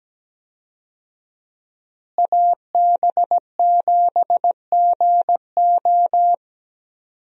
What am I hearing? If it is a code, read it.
Morse code, heard as a keyed tone, about 17 wpm, 700 Hz. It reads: AB7GO
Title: USA Call Signs - 5 Characters - Repeated 2x - 17wpm